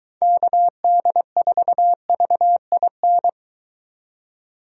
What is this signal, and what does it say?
Morse code, heard as a keyed tone, about 23 wpm, 700 Hz. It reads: KD4VIN